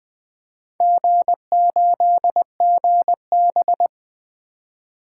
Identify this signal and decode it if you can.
Morse code, heard as a keyed tone, about 21 wpm, 700 Hz. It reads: G8GB
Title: Four Letter-Number Combinations - 20wpm